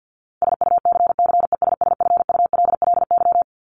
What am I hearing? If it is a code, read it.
Morse code, heard as a keyed tone, about 50 wpm, 700 Hz. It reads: SUCCESSFULLY
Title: LY Suffix - Top 100 Words with Base Words 50wpm